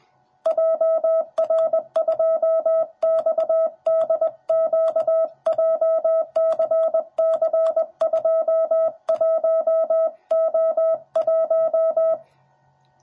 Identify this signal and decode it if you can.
Morse code, heard as a keyed tone, about 20 wpm, 650 Hz. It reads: JR2XDQ JCC21O1
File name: jh8cbh / 2021cw / cw10b